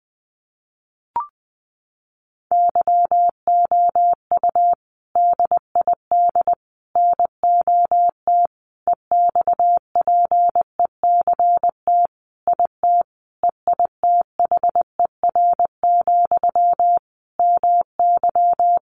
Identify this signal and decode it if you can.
Morse code, heard as a keyed tone, about 20 wpm, 700 Hz. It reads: YOU DID NOT EXPECT IT EITHER, MY
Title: Sentences from Top 1100 Words 20wpm